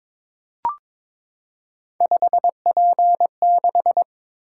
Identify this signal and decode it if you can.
Morse code, heard as a keyed tone, about 22 wpm, 700 Hz. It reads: 5P6
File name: practice / 22wpm / Three Letter-Number Combinations - 22wpm